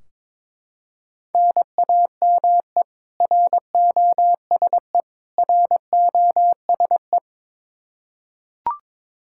Morse code, heard as a keyed tone, about 22 wpm, 700 Hz. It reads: NAME ROSE ROSE
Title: Practice QSO - Exchange 1 - Beginning - 22wpm